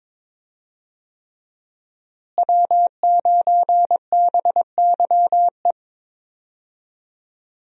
Morse code, heard as a keyed tone, about 22 wpm, 700 Hz. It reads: W9BYE
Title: USA Call Signs - 5 Characters - Repeated 3x - 22wpm